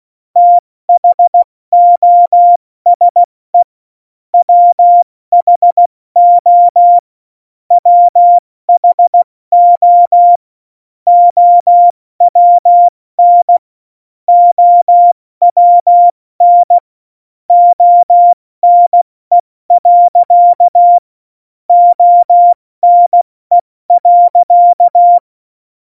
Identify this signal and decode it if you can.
Morse code, heard as a keyed tone, about 16 wpm, 700 Hz. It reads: THOSE WHO WHO OWN OWN ONE. ONE.
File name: Morsekurs / Witze / Witz 25 Juli 17, Tempo 70 F80